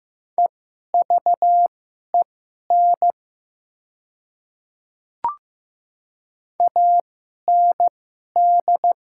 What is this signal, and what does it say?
Morse code, heard as a keyed tone, about 15 wpm, spaced as for 11 wpm, 700 Hz. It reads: EVEN AND
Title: 2x Character Spacing - Top 200 Words - 15wpm